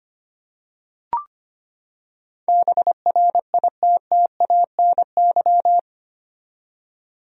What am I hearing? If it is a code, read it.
Morse code, heard as a keyed tone, about 25 wpm, 700 Hz. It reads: BRITTANY
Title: USA Common Names - 25wpm